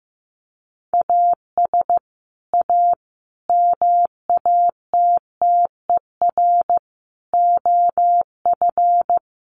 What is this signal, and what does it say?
Morse code, heard as a keyed tone, about 15 wpm, 700 Hz. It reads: AS A MATTER OF
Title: Sets of 5 Words 15wpm